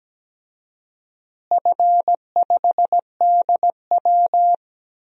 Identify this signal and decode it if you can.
Morse code, heard as a keyed tone, about 17 wpm, 700 Hz. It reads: F5DW
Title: DXCC-100 Call Signs - 4 Characters - Repeated 3x - 17wpm